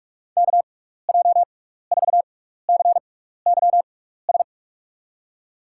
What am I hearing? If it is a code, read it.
Morse code, heard as a keyed tone, about 44 wpm, spaced as for 15 wpm, 700 Hz. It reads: KJ4CYS
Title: USA Call Signs - 6 Characters - 6x Character Spacing - 45wpm